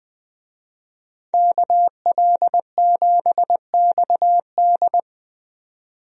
Morse code, heard as a keyed tone, about 20 wpm, 700 Hz. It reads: KL7XD
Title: DXCC-100 Call Signs - 5 Characters - 20wpm